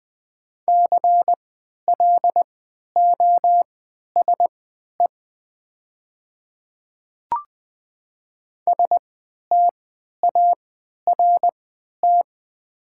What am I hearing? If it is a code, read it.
Morse code, heard as a keyed tone, about 20 wpm, spaced as for 11 wpm, 700 Hz. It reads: CLOSE START
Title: 3x Character Spacing - Top 300 Words - 20wpm